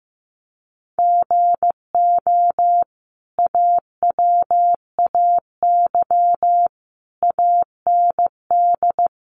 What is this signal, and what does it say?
Morse code, heard as a keyed tone, about 15 wpm, 700 Hz. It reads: GO AWAY AND
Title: Sentences from Top 1300 Words 15wpm